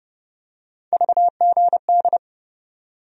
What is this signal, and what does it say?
Morse code, heard as a keyed tone, about 30 wpm, 700 Hz. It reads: VGD
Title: USA Call Sign Postfixes - 3 Letters 30wpm